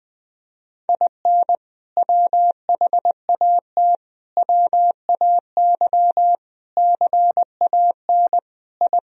Morse code, heard as a keyed tone, about 20 wpm, 700 Hz. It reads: IN WHAT WAY CAN I
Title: Sentences from Top 1000 Words - Review Part 4 - 20wpm